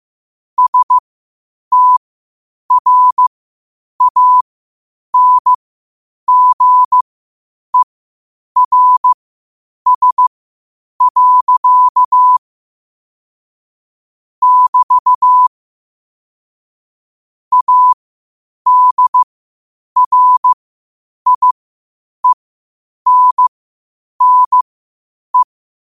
Morse code, heard as a keyed tone, about 15 wpm, spaced as for 8 wpm, 1000 Hz. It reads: STRANGERS. <BT> ADRIENNE
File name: qotdfiles / 8wpm